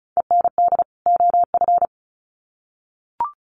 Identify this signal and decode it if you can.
Morse code, heard as a keyed tone, about 34 wpm, 700 Hz. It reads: END OF
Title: Sets of 2 Words 35wpm